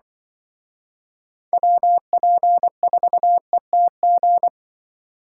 Morse code, heard as a keyed tone, about 24 wpm, 700 Hz. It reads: WP4ETG
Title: Speed Racing - USA 2x3 Call Signs 20wpm